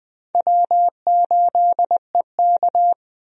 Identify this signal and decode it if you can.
Morse code, heard as a keyed tone, about 20 wpm, 700 Hz. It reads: W8EK
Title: USA Call Signs - 4 Characters - Repeated 2x - 20wpm